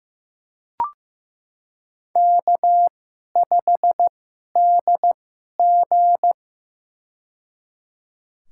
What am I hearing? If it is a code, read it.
Morse code, heard as a keyed tone, about 15 wpm, spaced as for 11 wpm, 700 Hz. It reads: K5DG
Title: USA Call Signs - 2x Character Spacing - 15wpm